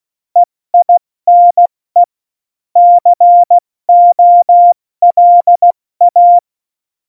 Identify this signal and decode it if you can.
Morse code, heard as a keyed tone, about 16 wpm, 700 Hz. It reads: EINE COLA
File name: Morsekurs / Witze / Witz 10.Dezember 2019, 70 F80